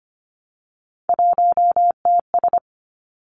Morse code, heard as a keyed tone, about 25 wpm, 700 Hz. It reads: 1TS
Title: Three Challenging Letter-Number Combinations 25wpm